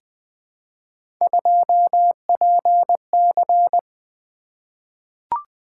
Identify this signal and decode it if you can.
Morse code, heard as a keyed tone, about 20 wpm, 700 Hz. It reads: 2PC
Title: Three Letter-Number Combinations - 20wpm